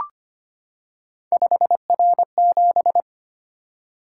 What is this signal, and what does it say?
Morse code, heard as a keyed tone, about 25 wpm, 700 Hz. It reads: 5R7
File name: Three Letter-Number Combinations - 25wpm